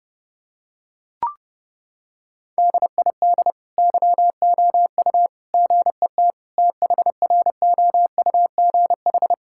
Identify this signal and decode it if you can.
Morse code, heard as a keyed tone, about 32 wpm, 700 Hz. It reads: DID YOU GET THROUGH
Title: Sentences from Top 1000 Words - Review Part 5 - 30wpm